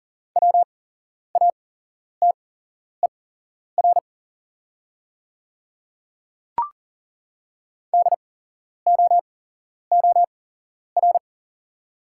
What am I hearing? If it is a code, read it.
Morse code, heard as a keyed tone, about 39 wpm, spaced as for 11 wpm, 690 Hz. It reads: WATER DOOR
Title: 8x Character Spacing - Top 300 Words - 40wpm